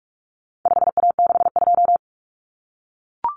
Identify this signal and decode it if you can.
Morse code, heard as a keyed tone, about 44 wpm, 700 Hz. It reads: 5A62
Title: Common Cut Numbers - 4-Digit 45wpm